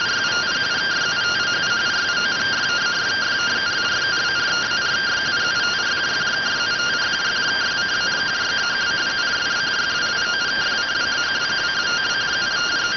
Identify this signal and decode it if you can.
Two-tone teleprinter signal, about 170 Hz shift.